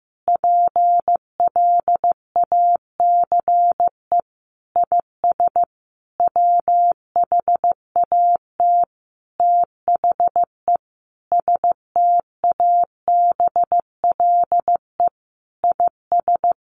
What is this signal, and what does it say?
Morse code, heard as a keyed tone, about 15 wpm, 700 Hz. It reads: PLACE IS WHAT THE STABLE IS
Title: Sentences from Top 1500 Words 15wpm